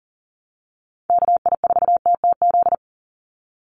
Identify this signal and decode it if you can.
Morse code, heard as a keyed tone, about 40 wpm, 700 Hz. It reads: KI4TTZ
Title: USA Call Signs 40wpm